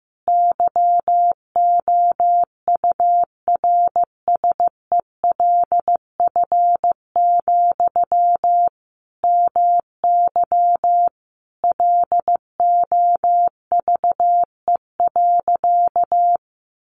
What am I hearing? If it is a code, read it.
Morse code, heard as a keyed tone, about 15 wpm, 700 Hz. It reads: YOURSELF, MY LOVE.
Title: Sentences from Top 1100 Words 15wpm